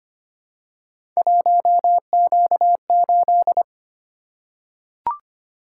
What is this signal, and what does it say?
Morse code, heard as a keyed tone, about 25 wpm, 700 Hz. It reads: 1Q8
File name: Three Letter-Number Combinations - 25wpm